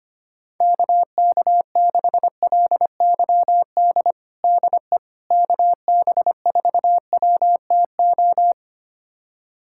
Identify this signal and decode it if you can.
Morse code, heard as a keyed tone, about 25 wpm, 700 Hz. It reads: KK6LYD DE KB4WTO
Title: USA Callsign DE Callsign - 6 Characters - 25wpm